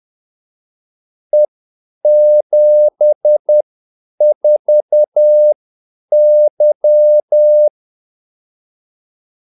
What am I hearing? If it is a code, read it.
Morse code, heard as a keyed tone, about 10 wpm, spaced as for 8 wpm, 600 Hz. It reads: E74Y
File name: Callsigns-10-8wpm